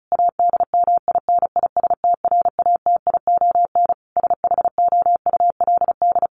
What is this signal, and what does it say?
Morse code, heard as a keyed tone, about 34 wpm, 700 Hz. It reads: ADMINISTRATION SHOULD
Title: Sentences from Top 1300 Words 35wpm